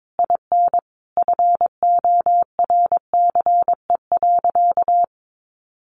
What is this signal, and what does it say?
Morse code, heard as a keyed tone, about 22 wpm, 700 Hz. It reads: IN FORCE.
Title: Sentences from Top 700 Words 22wpm